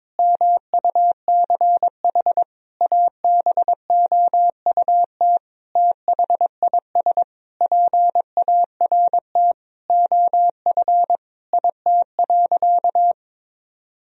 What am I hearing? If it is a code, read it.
Morse code, heard as a keyed tone, about 22 wpm, 700 Hz. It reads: MUCH ABOUT THIS PART OF IT.
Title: Sentences from Top 1000 Words - Review Part 1 - 22wpm